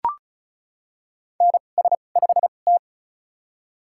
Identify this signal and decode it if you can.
Morse code, heard as a keyed tone, about 34 wpm, spaced as for 25 wpm, 700 Hz. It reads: NS5T
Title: USA Call Signs - 2x Character Spacing - 35wpm